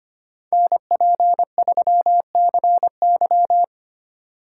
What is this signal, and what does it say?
Morse code, heard as a keyed tone, about 33 wpm, 700 Hz. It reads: NP3CY